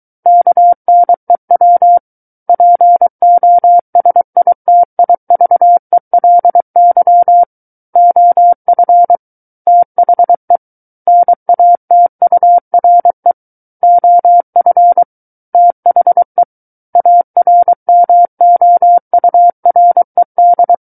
Morse code, heard as a keyed tone, about 23 wpm, 700 Hz. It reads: KNEW POSITIVELY OF THE NATURE OF THE ARMOURED
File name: Book One / 14-Chapter 14 - 23 WPM